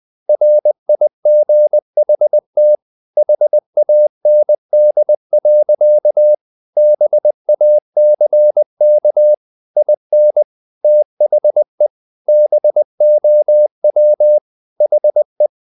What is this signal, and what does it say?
Morse code, heard as a keyed tone, about 20 wpm, 600 Hz. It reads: RIGHT HAND. BACK IN THE BOW HE
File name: the old man and the sea part 2 20wpm